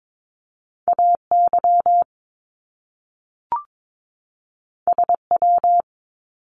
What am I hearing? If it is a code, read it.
Morse code, heard as a keyed tone, about 22 wpm, 700 Hz. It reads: AY SW
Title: Top Two Letter Combinations in Words 22wpm